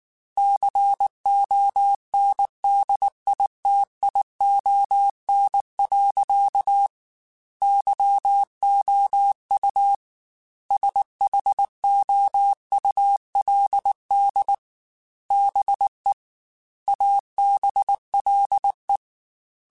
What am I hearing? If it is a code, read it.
Morse code, heard as a keyed tone, about 19 wpm, 800 Hz. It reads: CONDITION. YOU SHOULD BE ABLE
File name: Technical Subjects / Technical Subjects-mp3s / OR Gate @18